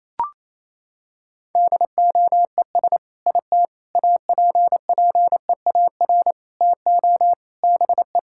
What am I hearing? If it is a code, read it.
Morse code, heard as a keyed tone, about 28 wpm, 700 Hz. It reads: DOES IT APPEAR TO BE
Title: Sentences from Top 1000 Words - Review Part 6 - 28wpm